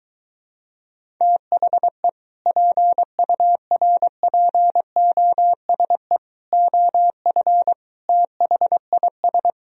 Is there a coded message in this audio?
Yes: Morse code from 1.2 to 9.5 s, heard as a keyed tone, about 23 wpm, 700 Hz. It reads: THE PURPOSE OF THIS